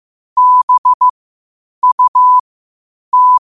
Morse code, heard as a keyed tone, about 15 wpm, spaced as for 8 wpm, 1000 Hz. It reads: BUT
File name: qotdfiles / 8wpm